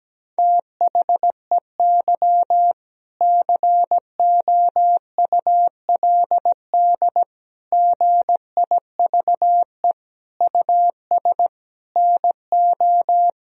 Morse code, heard as a keyed tone, about 17 wpm, 700 Hz. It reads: THEY COULD GIVE US NO
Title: Sentences from Top 1000 Words - Review Part 3 - 17wpm